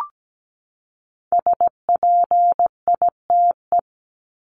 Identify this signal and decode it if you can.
Morse code, heard as a keyed tone, about 17 wpm, 700 Hz. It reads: SPITE
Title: Top 1751-2000 Words 17wpm